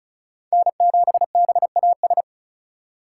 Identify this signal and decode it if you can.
Morse code, heard as a keyed tone, about 45 wpm, 700 Hz. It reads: N7BAS